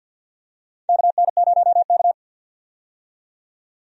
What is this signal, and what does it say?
Morse code, heard as a keyed tone, about 47 wpm, 700 Hz. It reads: KN0K
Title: USA Call Signs - 4 Characters - 50wpm